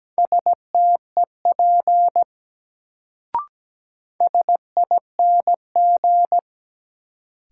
Morse code, heard as a keyed tone, about 17 wpm, 700 Hz. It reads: STEP SING
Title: Sentences from Top 400 Words - New Word Drill - 17wpm